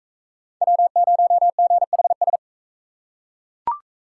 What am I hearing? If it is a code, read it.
Morse code, heard as a keyed tone, about 39 wpm, 700 Hz. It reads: W0GHS